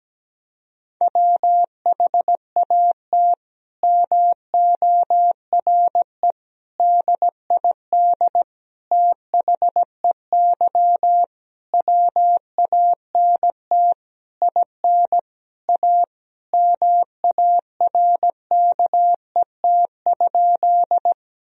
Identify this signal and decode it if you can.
Morse code, heard as a keyed tone, about 17 wpm, 700 Hz. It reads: WHAT MORE DID THEY WANT IN A MARKET?
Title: Sentences from Top 1000 Words - Review Part 3 - 17wpm